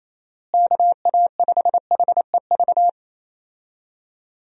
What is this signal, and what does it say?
Morse code, heard as a keyed tone, about 28 wpm, 700 Hz. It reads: KA5HEV
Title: USA Call Signs - 6 Characters - Repeated 3x - 28wpm